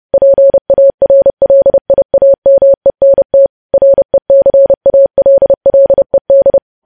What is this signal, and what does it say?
Morse code, heard as a keyed tone, about 30 wpm, 550 Hz. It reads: PARLIAMENT RECALLED
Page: news headlines in morse code at 30 wpm